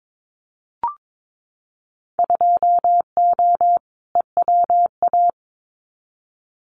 Morse code, heard as a keyed tone, about 22 wpm, 700 Hz. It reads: 2O EWA
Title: Winter Field Day Exchanges Encore 22wpm